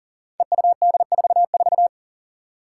Morse code, heard as a keyed tone, about 40 wpm, 700 Hz. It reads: EUD44
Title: Five Challenging Letter-Number Combinations 40wpm